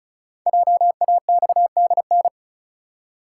Morse code, heard as a keyed tone, about 36 wpm, 700 Hz. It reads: JAXDN
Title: Code Groups - 5 Letters - 35wpm